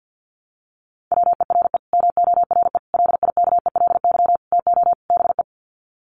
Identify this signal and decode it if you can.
Morse code, heard as a keyed tone, about 50 wpm, 700 Hz. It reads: WERE MORE LIKELY TO BE
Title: Sets of 5 Words Encore2 50wpm